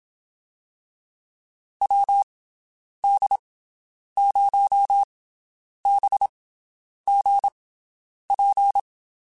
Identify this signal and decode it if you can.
Morse code, heard as a keyed tone, about 27 wpm, spaced as for 9 wpm, 800 Hz. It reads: WD0BGP
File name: (spaced) / QSOs Spaced-mp3s / WD0BGP 1 @25 sp